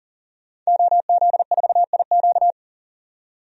Morse code, heard as a keyed tone, about 40 wpm, 700 Hz. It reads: OZ4IQ
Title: DXCC-100 Call Signs - 5 Characters - 40wpm